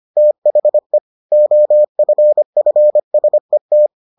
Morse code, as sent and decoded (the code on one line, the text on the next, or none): - .... / --- ..-. ..-. ... -
THE OFFSET